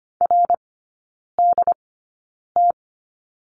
Morse code, heard as a keyed tone, about 25 wpm, spaced as for 9 wpm, 700 Hz. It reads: RDT
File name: Lesson 17e - Letter u - Three Characters 25wpm